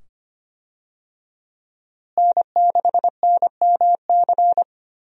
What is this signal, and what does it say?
Morse code, heard as a keyed tone, about 25 wpm, 700 Hz. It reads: N6NMC
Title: USA Call Signs - 5 Characters - Repeated 2x - 25wpm